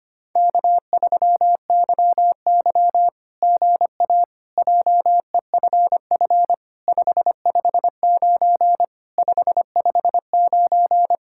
Morse code, heard as a keyed tone, about 25 wpm, 700 Hz. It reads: K3YY GA JEFF 559 559
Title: SOTA QSO Practice 25wpm